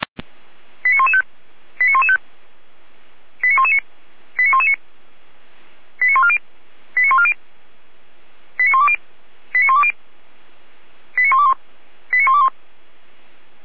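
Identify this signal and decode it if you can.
Five-tone selective call (ZVEI), heard as quick runs of five tones, tones of about 70 ms. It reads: 89115 89115 89119 89119 89140 89140 89120 89120 89121 89121
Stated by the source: ZVEI five-tone call received by SDR.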